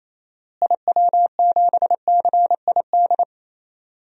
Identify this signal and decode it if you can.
Morse code, heard as a keyed tone, about 28 wpm, 700 Hz. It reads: IW7CID
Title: DXCC-100 Call Signs - 6 Characters - Repeated 2x - 28wpm